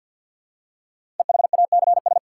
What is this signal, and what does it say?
Morse code, heard as a keyed tone, about 50 wpm, 700 Hz. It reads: EHACS
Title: Code Groups - 5 Letters - 50wpm